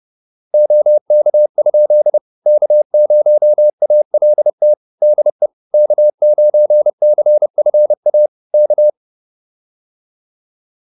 Morse code, heard as a keyed tone, about 30 wpm, 600 Hz. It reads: OK? K0ALT DE K9CFA K
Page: advanced-practice-files